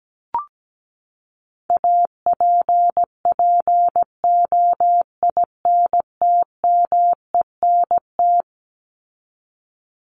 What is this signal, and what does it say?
Morse code, heard as a keyed tone, about 17 wpm, 700 Hz. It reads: APPOINTMENT
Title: Top 1751-2000 Words 17wpm